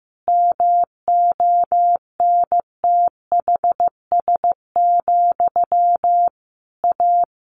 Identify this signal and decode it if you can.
Morse code, heard as a keyed tone, about 15 wpm, 700 Hz. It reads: MONTHS, A
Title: A Sound Of Thunder 15wpm